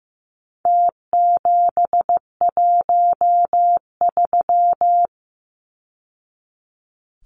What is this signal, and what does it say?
Morse code, heard as a keyed tone, about 15 wpm, 700 Hz. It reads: T713
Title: Numbers - Contest Style - 4-Digit 15wpm